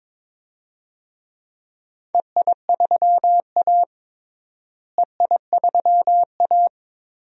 Morse code, heard as a keyed tone, about 21 wpm, 700 Hz. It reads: EI3A EI3A
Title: DXCC-100 Call Signs - 4 Characters - Repeated 2x - 22wpm